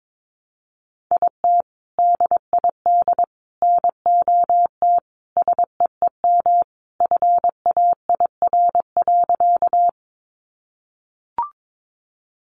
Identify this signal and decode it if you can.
Morse code, heard as a keyed tone, about 22 wpm, 700 Hz. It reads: IT DID NOT SEEM FAIR.
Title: Sentences from Top 1300 Words 22wpm